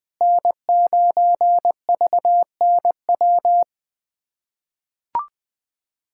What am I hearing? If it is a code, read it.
Morse code, heard as a keyed tone, about 20 wpm, 700 Hz. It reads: N9VNW
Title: USA Callsign DE Callsign - 5 Characters - 20wpm